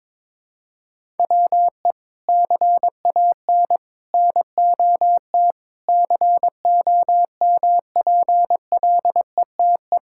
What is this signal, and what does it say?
Morse code, heard as a keyed tone, about 22 wpm, 700 Hz. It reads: WE CAN NOT COMPLETE
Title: Sentences from Top 1000 Words - Review Part 6 - 22wpm